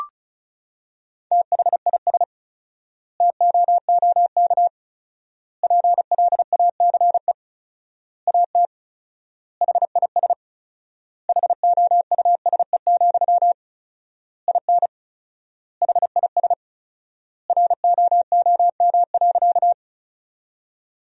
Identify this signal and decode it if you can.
Morse code, heard as a keyed tone, about 36 wpm, 700 Hz. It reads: THIS TOOK PLACE AT HIS HOUSE, IN HIS ROOM.
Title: 4x Word-Spacing - Sentences from Top 300 Words - 35wpm